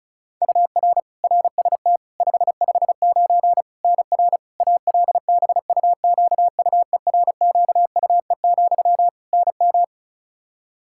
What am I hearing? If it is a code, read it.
Morse code, heard as a keyed tone, about 36 wpm, 700 Hz. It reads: UR RST 559 NR ALBUQUERQUE, NM
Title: Practice QSO - Exchange 1 - Beginning - 35wpm